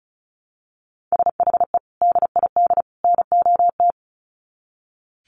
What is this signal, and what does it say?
Morse code, heard as a keyed tone, about 34 wpm, 700 Hz. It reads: SHE DID NOT
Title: Sets of 3 Words 35wpm